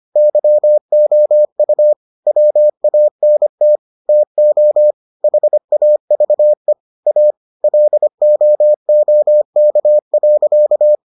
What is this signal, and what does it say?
Morse code, heard as a keyed tone, about 25 wpm, 600 Hz. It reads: YOU WANT TO HAVE A LOOK.